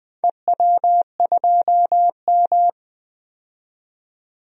Morse code, heard as a keyed tone, about 20 wpm, 700 Hz. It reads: EW2M
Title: DXCC-100 Call Signs - 4 Characters - 20wpm